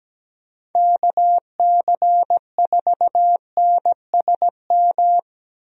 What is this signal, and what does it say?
Morse code, heard as a keyed tone, about 17 wpm, 700 Hz. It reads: KC4NSM
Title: USA Call Signs - 6 Characters - 17wpm